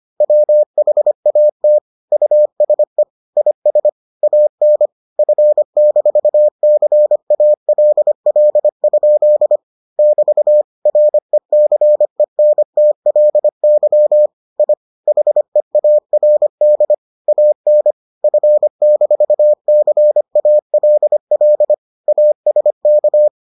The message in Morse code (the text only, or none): WHAT USE IS AN F-CALL? <BT> RECENTLY I HEARD AN F-CALL ASK